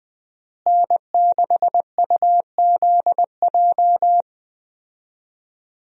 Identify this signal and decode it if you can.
Morse code, heard as a keyed tone, about 21 wpm, 700 Hz. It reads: N6UZJ
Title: USA Call Signs - 5 Characters - Repeated 3x - 20wpm